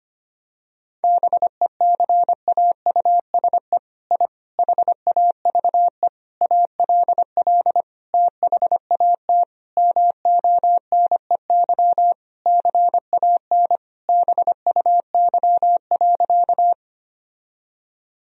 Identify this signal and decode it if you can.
Morse code, heard as a keyed tone, about 25 wpm, 700 Hz. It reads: BECAUSE I HAVE ALL THAT MONEY CAN BUY.
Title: Sentences from Top 1000 Words - Review Part 4 - 25wpm